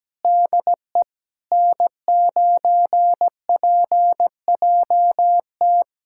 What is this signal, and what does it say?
Morse code, heard as a keyed tone, about 17 wpm, 700 Hz. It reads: DE N9PJT
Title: USA Callsign DE Callsign - 5 Characters - 17wpm